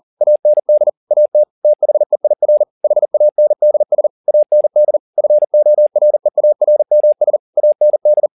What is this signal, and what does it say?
Morse code, heard as a keyed tone, about 40 wpm, 600 Hz. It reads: AND AT THEIR HANDS AND FOREARMS AND
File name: the old man and the sea 2 40wpm